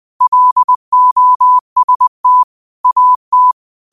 Morse code, heard as a keyed tone, about 20 wpm, 1000 Hz. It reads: LOST AT
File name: qotdfiles / 20wpm